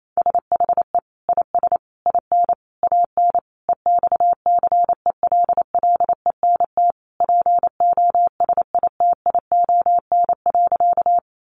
Morse code, heard as a keyed tone, about 28 wpm, 700 Hz. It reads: SHE IS IN AN EXCELLENT POSITION.
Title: Sentences from Top 1000 Words 28wpm